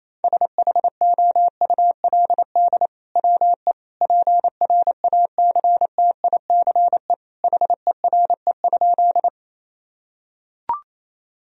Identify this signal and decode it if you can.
Morse code, heard as a keyed tone, about 28 wpm, 700 Hz. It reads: SHOULD WE PRACTICE HERE?
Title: Sentences from Top 1000 Words - Review Part 4 - 28wpm